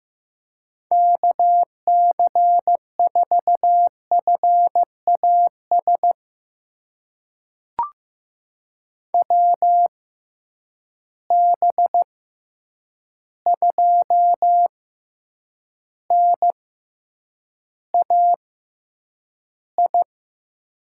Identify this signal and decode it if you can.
Morse code, heard as a keyed tone, about 15 wpm, spaced as for 8 wpm, 700 Hz. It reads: KC4FAS WB2NAI
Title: USA Call Signs - 6 Characters - 6x Character Spacing - 15wpm